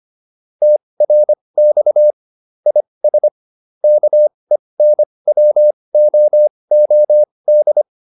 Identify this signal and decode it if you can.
Morse code, heard as a keyed tone, about 25 wpm, spaced as for 20 wpm, 600 Hz. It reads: TRX IS KENWOOD
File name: QSO201a 20